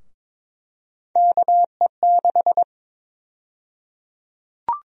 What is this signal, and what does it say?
Morse code, heard as a keyed tone, about 22 wpm, 700 Hz. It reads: KE6
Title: USA Call Sign Prefixes - 22wpm